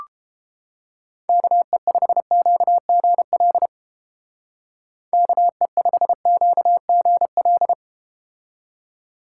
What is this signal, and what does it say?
Morse code, heard as a keyed tone, about 30 wpm, 700 Hz. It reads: KE5QGL KE5QGL